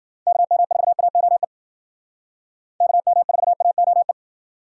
Morse code, heard as a keyed tone, about 60 wpm, 700 Hz. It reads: KM4AOE KM4AOE